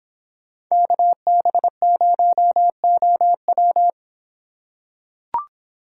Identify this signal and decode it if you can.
Morse code, heard as a keyed tone, about 26 wpm, 700 Hz. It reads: KB0OW